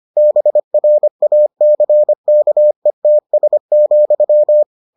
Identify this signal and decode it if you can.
Morse code, heard as a keyed tone, about 25 wpm, 600 Hz. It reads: BRACKETS,